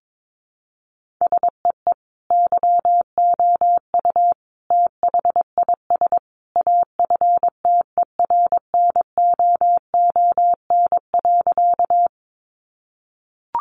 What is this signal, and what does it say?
Morse code, heard as a keyed tone, about 22 wpm, 700 Hz. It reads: SEE YOU THIS AFTERNOON.
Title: Sentences from Top 1500 Words 22wpm